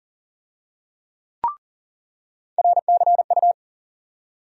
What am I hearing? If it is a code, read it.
Morse code, heard as a keyed tone, about 38 wpm, 700 Hz. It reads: RCU
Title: USA Call Sign Postfixes - 3 Letters 40wpm